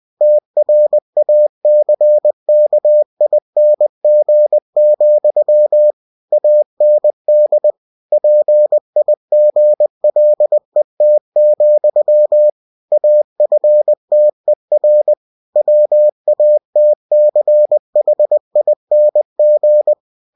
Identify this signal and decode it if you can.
Morse code, heard as a keyed tone, about 20 wpm, 600 Hz. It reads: TRACKING, AND PIGLET, AFTER WATCHING
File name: cw-books / winnie-the-pooh / winnie-20wpm-page-0022